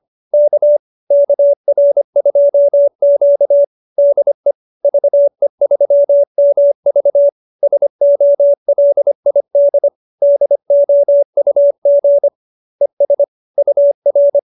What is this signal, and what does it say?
Morse code, heard as a keyed tone, about 25 wpm, 570 Hz. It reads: K KR2Q DE VE3MV SOLID DOUG ES UR
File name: QSO220 25